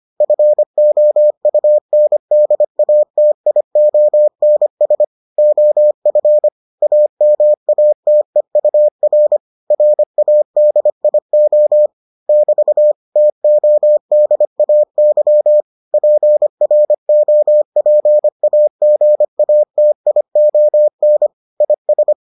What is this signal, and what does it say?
Morse code, heard as a keyed tone, about 25 wpm, 600 Hz. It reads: FOUNDATIONS OF AMATEUR RADIO <BT> TODAY PROPAGATION IS